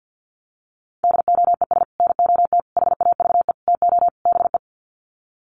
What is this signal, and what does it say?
Morse code, heard as a keyed tone, about 49 wpm, 700 Hz. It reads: DOES NOT HAVE TO BE
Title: Sets of 5 Words 50wpm